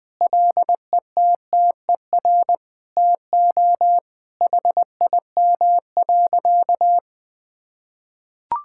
Morse code, heard as a keyed tone, about 21 wpm, 700 Hz. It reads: LETTER TO HIM.
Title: Sentences with 2-Word Phrases - 20wpm